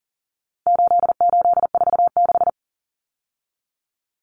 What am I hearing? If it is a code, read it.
Morse code, heard as a keyed tone, about 40 wpm, 700 Hz. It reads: 8846
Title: Numbers - 4-Digit 40wpm